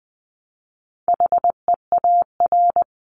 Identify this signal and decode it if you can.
Morse code, heard as a keyed tone, about 20 wpm, 700 Hz. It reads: HEAR
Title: Top 300 Words Repeated 20wpm